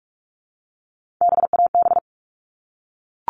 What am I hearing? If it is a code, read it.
Morse code, heard as a keyed tone, about 44 wpm, 700 Hz. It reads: BAB
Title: All Cut Numbers - 3-Digit 45wpm